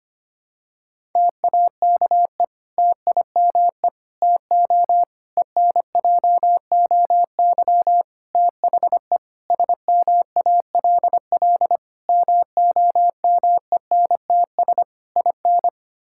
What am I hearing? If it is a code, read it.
Morse code, heard as a keyed tone, about 25 wpm, 700 Hz. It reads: TAKE TIME TO ENJOY THE SMALL MOMENTS IN
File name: practice / 25wpm / Difficult Sentences with Hints - 25wpm